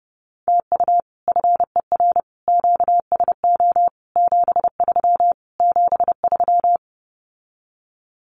Morse code, heard as a keyed tone, about 30 wpm, 700 Hz. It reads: TU FER QSO 73 73
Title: Common QSO Phrases Encore 30wpm